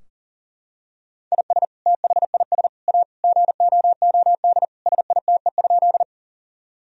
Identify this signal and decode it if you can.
Morse code, heard as a keyed tone, about 40 wpm, 700 Hz. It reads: IS THIS A GOOD SITE?
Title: Sentences from Top 1000 Words - Review Part 4 - 40wpm